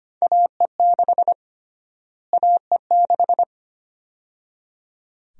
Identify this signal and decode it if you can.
Morse code, heard as a keyed tone, about 25 wpm, 700 Hz. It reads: AE6 AE6
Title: USA Call Sign Prefixes - Repeated 2x - 25wpm